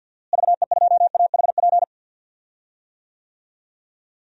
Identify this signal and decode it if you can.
Morse code, heard as a keyed tone, about 53 wpm, 700 Hz. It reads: VE2AHP
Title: DXCC-100 Call Signs - 6 Characters - Repeated 2x - 50wpm